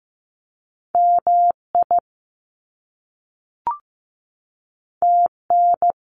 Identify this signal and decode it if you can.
Morse code, heard as a keyed tone, about 15 wpm, 700 Hz. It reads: MI TN
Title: US States 15wpm